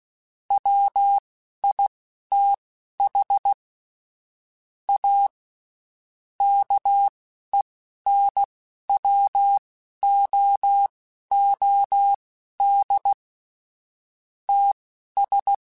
Morse code, heard as a keyed tone, about 16 wpm, spaced as for 11 wpm, 800 Hz. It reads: WITH A KENWOOD TS
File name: QSOs / QSOs-mp3s / WD0BGP 1@15